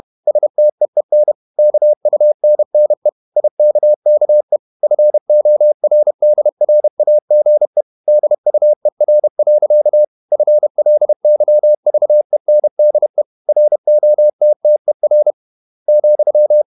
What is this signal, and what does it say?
Morse code, heard as a keyed tone, about 31 wpm, 600 Hz. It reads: STEEN KUNNE IKKE FORDRAGE DUER. FLYVENDE ROTTER,
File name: Ferie QRQ 31wpm